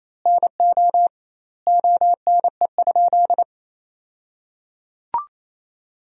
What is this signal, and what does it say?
Morse code, heard as a keyed tone, about 28 wpm, 700 Hz. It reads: NO ONE?
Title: 2x Word-Spacing - Sentences from Top 300 Words - 28wpm